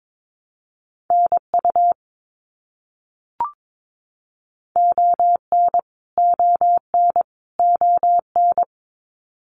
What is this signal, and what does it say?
Morse code, heard as a keyed tone, about 22 wpm, 700 Hz. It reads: NU ON ON ON
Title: Canadian Provinces and Territories - Repeated 3x - 22wpm